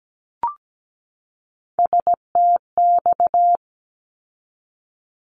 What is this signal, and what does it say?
Morse code, heard as a keyed tone, about 17 wpm, 700 Hz. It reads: STX
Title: ARRL Sections 17wpm